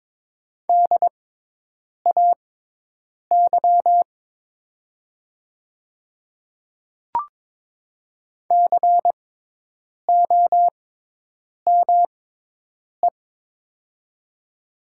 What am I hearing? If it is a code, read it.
Morse code, heard as a keyed tone, about 22 wpm, spaced as for 8 wpm, 700 Hz. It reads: DAY COME
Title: 6x Character Spacing - Top 100 Words - 22wpm